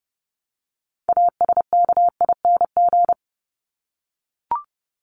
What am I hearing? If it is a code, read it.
Morse code, heard as a keyed tone, about 30 wpm, 700 Hz. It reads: ASKING
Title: ING Suffix - Top 100 Words with Base Words 30wpm